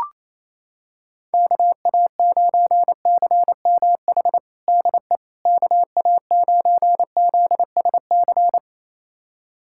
Morse code, heard as a keyed tone, about 28 wpm, 700 Hz. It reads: KA9CMH DE KA9ZSC
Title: USA Callsign DE Callsign 28wpm